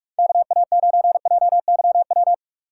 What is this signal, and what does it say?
Morse code, heard as a keyed tone, about 45 wpm, 690 Hz. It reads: KA9JYW
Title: USA Call Signs - 6 Characters - 3x Character Spacing - 45wpm